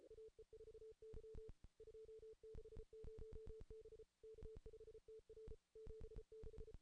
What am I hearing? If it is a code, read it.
Morse code, heard as a keyed tone, about 34 wpm, 440 Hz. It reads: AE4Y 260 B K5TR 76
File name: contests / ss-cw-k5tr-op-n5rz -2300